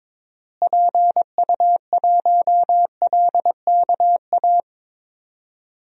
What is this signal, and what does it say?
Morse code, heard as a keyed tone, about 22 wpm, 700 Hz. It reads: PU1LKA
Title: DXCC-100 Call Signs - 6 Characters - Repeated 3x - 22wpm